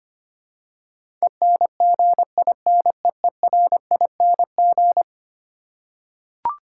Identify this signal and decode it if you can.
Morse code, heard as a keyed tone, about 25 wpm, 700 Hz. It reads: ENGINEERING
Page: morse-code-ninja-practice-25wpm